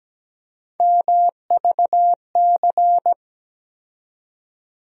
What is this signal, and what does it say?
Morse code, heard as a keyed tone, about 17 wpm, 700 Hz. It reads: MVC
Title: USA Call Sign Postfixes - 3 Letters 17wpm